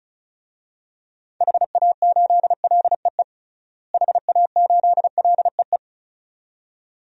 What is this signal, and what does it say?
Morse code, heard as a keyed tone, about 34 wpm, 700 Hz. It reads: HA8LEE HA8LEE
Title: DXCC-100 Call Signs - 6 Characters - Repeated 2x - 35wpm